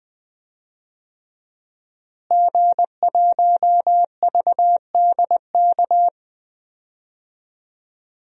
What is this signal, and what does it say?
Morse code, heard as a keyed tone, about 20 wpm, 700 Hz. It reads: G1VDK
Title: DXCC-100 Call Signs - 5 Characters - Repeated 3x - 20wpm